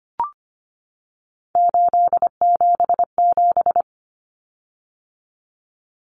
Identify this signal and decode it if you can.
Morse code, heard as a keyed tone, about 25 wpm, 700 Hz. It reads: 877
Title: Common Cut Numbers - 3-Digit 25wpm